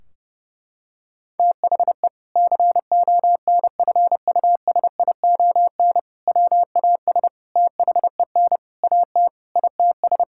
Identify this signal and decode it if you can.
Morse code, heard as a keyed tone, about 30 wpm, 700 Hz. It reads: THE CONFUSION WAS THEN AT ITS
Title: Sentences from Top 1600 Words 30wpm